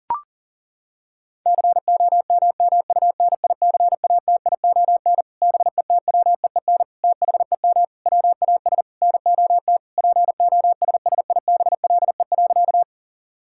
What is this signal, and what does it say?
Morse code, heard as a keyed tone, about 40 wpm, 700 Hz. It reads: COMMUNICATION BETWEEN THEM WAS NOT POSSIBLE.
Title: Sentences from Top 1000 Words - Review Part 6 - 40wpm